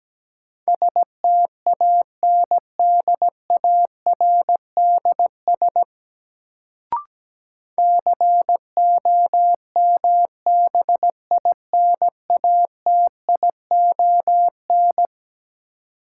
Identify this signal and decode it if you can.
Morse code, heard as a keyed tone, about 17 wpm, 700 Hz. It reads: STANDARDS COMBINATION
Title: Sentences from Top 1000 Words - Review Part 6 - New Word Drill - 17wpm